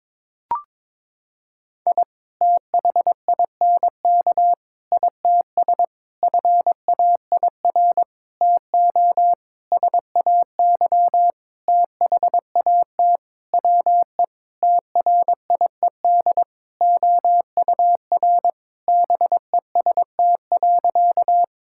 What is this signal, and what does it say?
Morse code, heard as a keyed tone, about 21 wpm, 700 Hz. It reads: I THINK ITS FAIR TO SAY THAT WE TRIED OUR BEST.
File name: Sentences with 7-Word Phrases - 22wpm